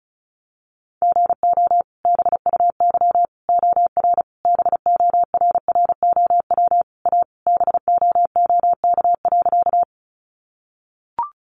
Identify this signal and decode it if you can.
Morse code, heard as a keyed tone, about 34 wpm, 700 Hz. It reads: GO BUY OR BORROW A BOOK.
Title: Sentences from Top 1500 Words 35wpm